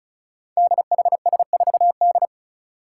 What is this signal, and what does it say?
Morse code, heard as a keyed tone, about 36 wpm, 700 Hz. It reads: DHS4D
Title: Five Challenging Letter-Number Combinations 35wpm